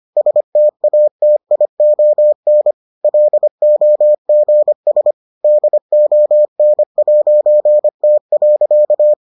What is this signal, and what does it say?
Morse code, heard as a keyed tone, about 25 wpm, 600 Hz. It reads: STATION LOGS DON'T.